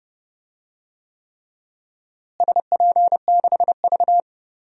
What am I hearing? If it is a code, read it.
Morse code, heard as a keyed tone, about 30 wpm, 700 Hz. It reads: SP6V
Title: DXCC-100 Call Signs - 4 Characters - Repeated 3x - 30wpm